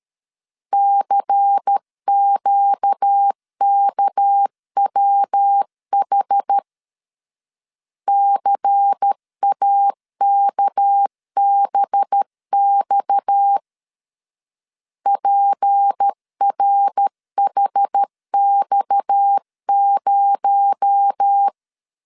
Morse code, heard as a keyed tone, about 13 wpm, 800 Hz. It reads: CQKWH CAKBX PRHX0
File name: CW / DET / 1 6 1500C S C 12wpm 3533